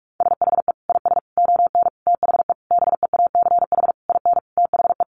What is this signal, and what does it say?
Morse code, heard as a keyed tone, about 44 wpm, 700 Hz. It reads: SHE IS ON THE BEACH IN THE
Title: Sentences from Top 1500 Words 45wpm